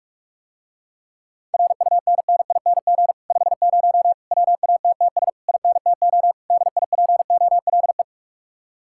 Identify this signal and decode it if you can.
Morse code, heard as a keyed tone, about 49 wpm, 700 Hz. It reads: RUNNING 50 WATTS INTO DIPOLE